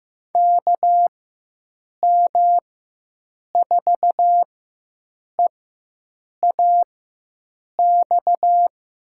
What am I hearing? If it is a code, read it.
Morse code, heard as a keyed tone, about 15 wpm, spaced as for 7 wpm, 700 Hz. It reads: KM4EAX
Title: USA Call Signs - 6 Characters - 4x Character Spacing - 15wpm